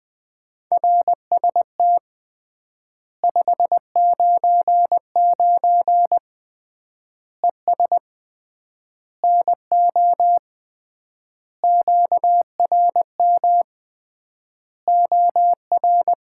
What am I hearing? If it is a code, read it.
Morse code, heard as a keyed tone, about 20 wpm, 700 Hz. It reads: RST 599 ES NO QRM OR
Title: Common QSO Phrases - 3x Word-Spacing - 20wpm